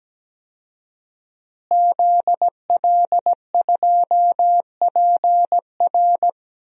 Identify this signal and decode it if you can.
Morse code, heard as a keyed tone, about 17 wpm, 700 Hz. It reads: ZL2PR
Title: DXCC-100 Call Signs - 5 Characters - Repeated 3x - 17wpm